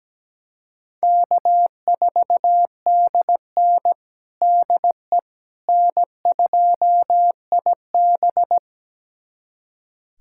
Morse code, heard as a keyed tone, about 17 wpm, 700 Hz. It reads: K4DN DE N2IB
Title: USA Callsign DE Callsign - 4 Characters - 17wpm